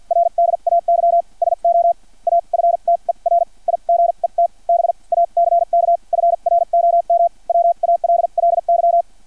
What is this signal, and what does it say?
Morse code, heard as a keyed tone, about 50 wpm, spaced as for 40 wpm, 680 Hz. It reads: RDAY SO AFTER IGET BACK FROM WALLY
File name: archived / CRSnet